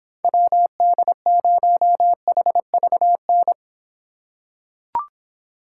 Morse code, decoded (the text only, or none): WD0HVN